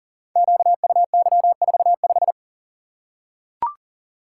Morse code, heard as a keyed tone, about 38 wpm, 700 Hz. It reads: QUY45
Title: Code Groups - 5 Letter:Numbers - 40wpm